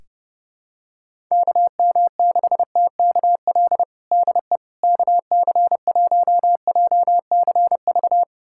Morse code, heard as a keyed tone, about 30 wpm, 700 Hz. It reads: KM6TKL DE KC1JCV
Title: USA Callsign DE Callsign 30wpm